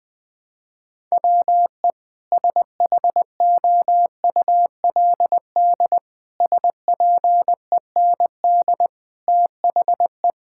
Morse code, heard as a keyed tone, about 21 wpm, 700 Hz. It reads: WE SHOULD SPEND THE
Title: Sentences from Top 1000 Words - Review Part 5 - 20wpm